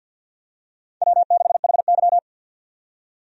Morse code, heard as a keyed tone, about 48 wpm, 700 Hz. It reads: W6HY
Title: USA Call Signs - 4 Characters - 50wpm